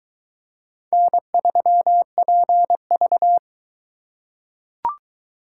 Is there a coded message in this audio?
Yes: Morse code from 0.9 to 3.4 s, heard as a keyed tone, about 22 wpm, 700 Hz. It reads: N3PV